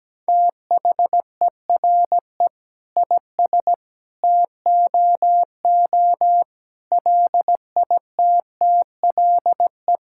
Morse code, heard as a keyed tone, about 17 wpm, 700 Hz. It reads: THERE IS TOO LITTLE
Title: Sentences from Top 1000 Words - Review Part 5 - 17wpm